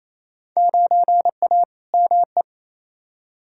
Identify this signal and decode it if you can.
Morse code, heard as a keyed tone, about 28 wpm, 700 Hz. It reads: 9A ME
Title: Field Day Exchanges Encore - 28wpm